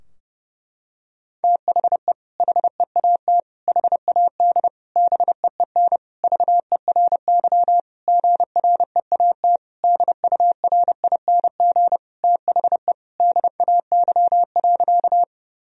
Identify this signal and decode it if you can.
Morse code, heard as a keyed tone, about 30 wpm, 700 Hz. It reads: THE HEAT HAD BEEN VERY GREAT DURING THE DAY.
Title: Sentences from Top 1000 Words - Review Part 3 - 30wpm